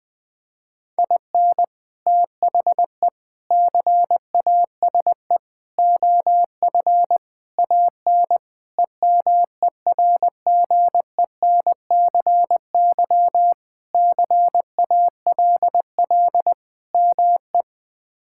Morse code, heard as a keyed tone, about 20 wpm, 700 Hz. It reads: IN THE CASE OF AN EMERGENCY CALL ME
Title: Sentences with 3-Word Phrases - Spoken Beginning - 20wpm